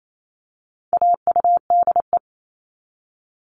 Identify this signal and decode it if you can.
Morse code, heard as a keyed tone, about 28 wpm, 700 Hz. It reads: AUDE
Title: All Cut Numbers - 4-Digit 28wpm